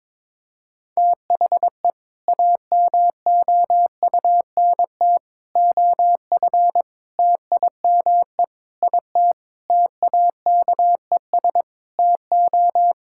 Morse code, heard as a keyed tone, about 22 wpm, 700 Hz. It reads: THE AMOUNT OF TIME IT TAKES TO
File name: Sentences with 7-Word Phrases - 22wpm